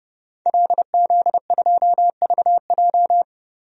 Morse code, heard as a keyed tone, about 30 wpm, 700 Hz. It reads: LZ2VJ
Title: DXCC-100 Call Signs - 5 Characters - Repeated 2x - 30wpm